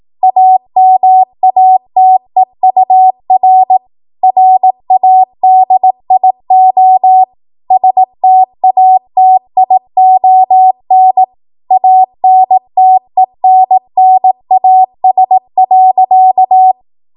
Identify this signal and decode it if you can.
Morse code, heard as a keyed tone, about 18 wpm, 750 Hz. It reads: AMATEUR RADIO STATION ANTENNAS.
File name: Morse / 18WPM